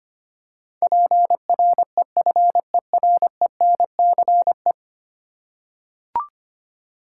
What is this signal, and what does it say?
Morse code, heard as a keyed tone, about 25 wpm, 700 Hz. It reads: PREFERENCE
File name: 10 Letter Words - First 3 Letters Given - 25wpm